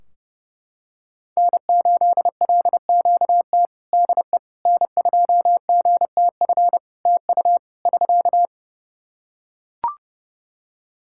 Morse code, heard as a keyed tone, about 30 wpm, 700 Hz. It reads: N8LQT DE N2GTF TU <SK>
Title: K1USN SST QSO Practice 30wpm